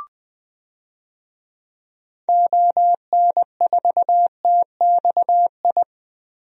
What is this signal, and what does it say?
Morse code, heard as a keyed tone, about 21 wpm, 700 Hz. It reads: ON4TXI
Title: DXCC-100 Call Signs - 6 Characters - 20wpm